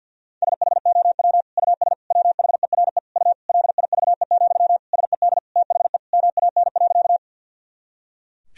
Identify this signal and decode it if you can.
Morse code, heard as a keyed tone, about 50 wpm, 700 Hz. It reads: SHOW US WHERE U LIVE, SED THE MAN.
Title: Sentences from Top 200 Words and Abbreviations 50wpm